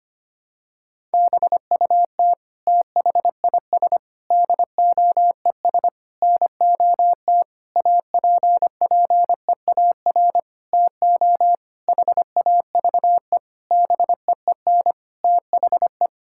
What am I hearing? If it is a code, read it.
Morse code, heard as a keyed tone, about 25 wpm, 700 Hz. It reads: BUT THIS DOES NOT APPEAR TO HAVE BEEN THE
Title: Sentences from Top 1000 Words - Review Part 6 - 25wpm